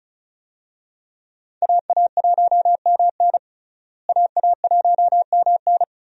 Morse code, heard as a keyed tone, about 36 wpm, 700 Hz. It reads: AA1MN AA1MN
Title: USA Call Signs - 5 Characters - Repeated 3x - 35wpm